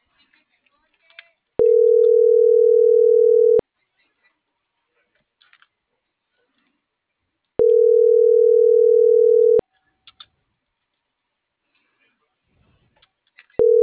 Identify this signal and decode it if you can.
Ringback tone (North America), 440 + 480 Hz.
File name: CENTRAL NMP / out erasto - .9632